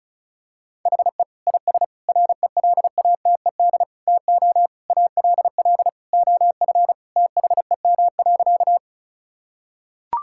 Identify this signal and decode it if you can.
Morse code, heard as a keyed tone, about 36 wpm, 700 Hz. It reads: HE IS RELATED TO ALL OF THEM.
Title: Sentences from Top 1000 Words - Review Part 5 - 35wpm